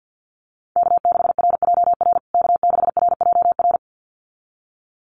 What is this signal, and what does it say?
Morse code, heard as a keyed tone, about 49 wpm, 700 Hz. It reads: K6RJR K6RJR
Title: USA Call Signs Repeated 2x 50wpm